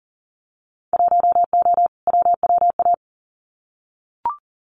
Morse code, heard as a keyed tone, about 42 wpm, 700 Hz. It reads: 1O WWA
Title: Winter Field Day Exchanges 40wpm